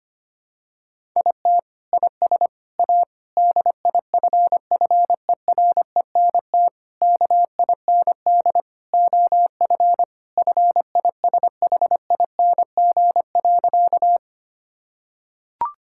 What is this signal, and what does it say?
Morse code, heard as a keyed tone, about 25 wpm, 700 Hz. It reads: IT IS A DIFFERENT KIND OF FISHING.
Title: Sentences from Top 1000 Words - Review Part 5 - 25wpm